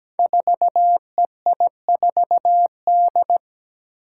Morse code, heard as a keyed tone, about 17 wpm, 700 Hz. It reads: 4EI4D
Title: Five Challenging Letter-Number Combinations 17wpm